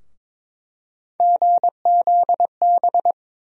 Morse code, heard as a keyed tone, about 22 wpm, 700 Hz. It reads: GZB
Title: USA Call Sign Postfixes - 3 Letters 22wpm